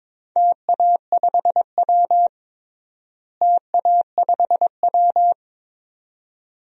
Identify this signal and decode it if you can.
Morse code, heard as a keyed tone, about 22 wpm, 700 Hz. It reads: TA5W TA5W
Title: DXCC-100 Call Signs - 4 Characters - Repeated 2x - 22wpm